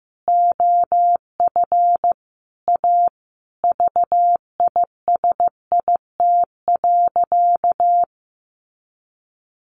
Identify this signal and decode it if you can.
Morse code, heard as a keyed tone, about 15 wpm, 700 Hz. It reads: OF A VISIT.
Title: Sentences from Top 1400 Words 15wpm